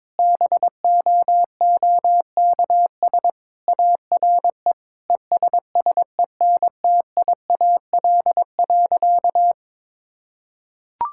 Morse code, heard as a keyed tone, about 22 wpm, 700 Hz. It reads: BOOKS ARE ESSENTIAL.
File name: practice / 22wpm / Sentences with 8-Word Phrases - 22wpm